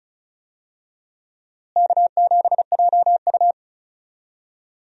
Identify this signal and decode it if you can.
Morse code, heard as a keyed tone, about 36 wpm, 700 Hz. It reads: K7JU
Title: USA Call Signs - 4 Characters - Repeated 2x - 35wpm